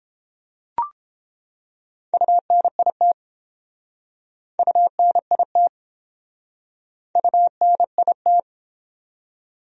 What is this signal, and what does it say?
Morse code, heard as a keyed tone, about 30 wpm, 700 Hz. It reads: UNIT UNIT UNIT